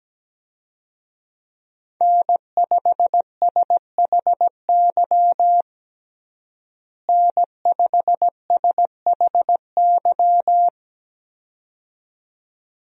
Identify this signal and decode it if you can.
Morse code, heard as a keyed tone, about 17 wpm, 700 Hz. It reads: N5SHY N5SHY
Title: USA Call Signs - 5 Characters - Repeated 2x - 17wpm